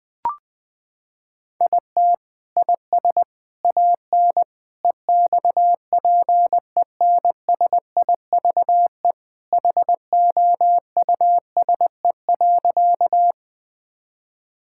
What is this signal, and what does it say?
Morse code, heard as a keyed tone, about 21 wpm, 700 Hz. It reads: IT IS AN EXPENSIVE HOUSE.
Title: Sentences from Top 1000 Words - Review Part 6 - 20wpm